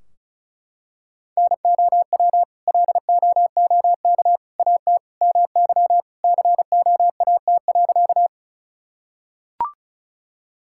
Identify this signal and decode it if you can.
Morse code, heard as a keyed tone, about 35 wpm, 700 Hz. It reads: NOW LOOK AT MY COAT.
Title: Sentences from Top 1600 Words 35wpm